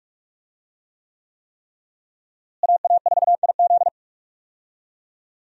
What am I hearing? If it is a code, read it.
Morse code, heard as a keyed tone, about 46 wpm, 700 Hz. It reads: AA4IZ
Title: USA Call Signs - 5 Characters - Repeated 3x - 45wpm